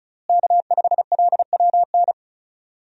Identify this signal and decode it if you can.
Morse code, heard as a keyed tone, about 36 wpm, 700 Hz. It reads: K5LWN
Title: USA Call Signs - 5 Characters - Repeated 3x - 35wpm